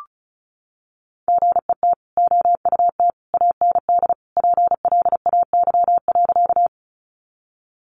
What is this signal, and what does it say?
Morse code, heard as a keyed tone, about 34 wpm, 700 Hz. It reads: GET OUT AND PLAY.
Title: Sentences from Top 200 Words 35wpm